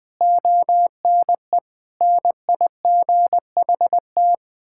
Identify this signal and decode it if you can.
Morse code, heard as a keyed tone, about 20 wpm, 700 Hz. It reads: ONE NIGHT
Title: Sentences from Top 1600 Words 20wpm